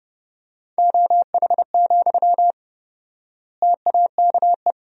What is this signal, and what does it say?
Morse code, heard as a keyed tone, about 30 wpm, 700 Hz. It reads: OH, TAKE
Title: 4x Word-Spacing - Sentences from Top 500 Words - 30wpm